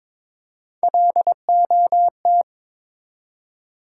Morse code, heard as a keyed tone, about 22 wpm, 700 Hz. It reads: LOT
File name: Build A Phrase - 2-Word Phrases - 22wpm